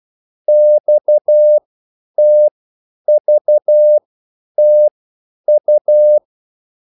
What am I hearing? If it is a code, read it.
Morse code, heard as a keyed tone, about 12 wpm, spaced as for 9 wpm, 600 Hz. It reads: XTVTU